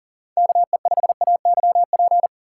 Morse code, heard as a keyed tone, about 40 wpm, 700 Hz. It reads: KE5AYP
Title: USA Call Signs - 4x Character Spacing - 40wpm